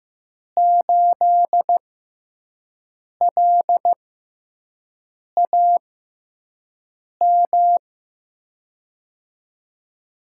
Morse code, heard as a keyed tone, about 15 wpm, spaced as for 5 wpm, 700 Hz. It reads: 8LAM
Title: USA Call Signs - 6 Characters - 6x Character Spacing - 15wpm